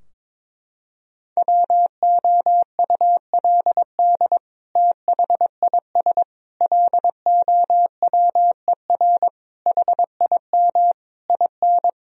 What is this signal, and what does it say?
Morse code, heard as a keyed tone, about 22 wpm, 700 Hz. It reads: WOULD THIS LOWER HIM IN
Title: Sentences from Top 1000 Words - Review Part 4 - 22wpm